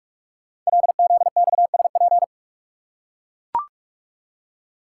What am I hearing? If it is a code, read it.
Morse code, heard as a keyed tone, about 44 wpm, 700 Hz. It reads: LZXSP
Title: Code Groups - 5 Letters - 45wpm